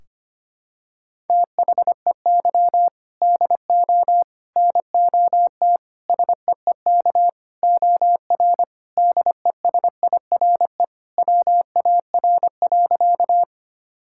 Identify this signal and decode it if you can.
Morse code, heard as a keyed tone, about 25 wpm, 700 Hz. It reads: THEY DO NOT SEEK OR DESIRE WAR.
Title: Sentences from Top 1600 Words 25wpm